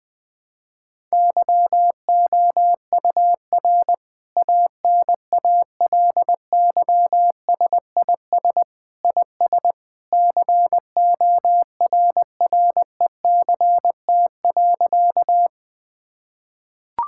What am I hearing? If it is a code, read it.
Morse code, heard as a keyed tone, about 20 wpm, 700 Hz. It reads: YOUR ANALYSIS IS CORRECT.
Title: Sentences from Top 1000 Words - Review Part 5 - 20wpm